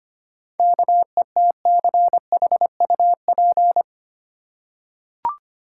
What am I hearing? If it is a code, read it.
Morse code, heard as a keyed tone, about 25 wpm, 700 Hz. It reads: KETCHUP
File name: practice / 25wpm / Easy Words with Hints - 25wpm